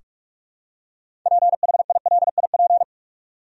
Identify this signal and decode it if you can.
Morse code, heard as a keyed tone, about 45 wpm, 700 Hz. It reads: PHILIP
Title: USA Common Names - 45wpm